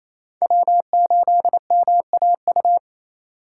Morse code, heard as a keyed tone, about 28 wpm, 700 Hz. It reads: W8MAU
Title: USA Call Signs - 5 Characters - 2x Character Spacing - 28wpm